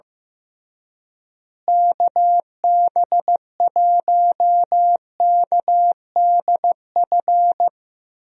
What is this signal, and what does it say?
Morse code, heard as a keyed tone, about 15 wpm, 700 Hz. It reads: KB1KDF
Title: USA Call Signs - 6 Characters - Repeated 3x - 15wpm